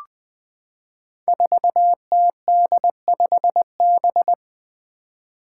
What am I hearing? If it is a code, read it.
Morse code, heard as a keyed tone, about 20 wpm, 700 Hz. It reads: 4TD5B
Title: Five Challenging Letter-Number Combinations 20wpm